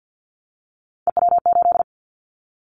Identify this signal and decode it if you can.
Morse code, heard as a keyed tone, about 50 wpm, 700 Hz. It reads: EW8
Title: Callsign Prefixes for Top 100 DXCC - 50wpm